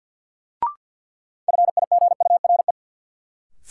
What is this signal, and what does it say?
Morse code, heard as a keyed tone, about 48 wpm, 700 Hz. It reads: FIGURE
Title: Sentences from Top 1400 Words - New Word Drill Alt - 50wpm